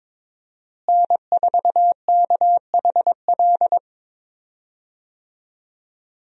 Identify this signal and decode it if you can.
Morse code, heard as a keyed tone, about 22 wpm, 700 Hz. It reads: N4KHL
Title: USA Call Signs - 5 Characters - Repeated 2x - 22wpm